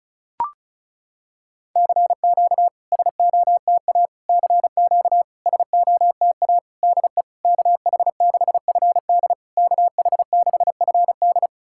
Morse code, heard as a keyed tone, about 36 wpm, 690 Hz. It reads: CQ SOTA CQ SOTA DE KH6FD KH6FD
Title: SOTA QSO Practice 35wpm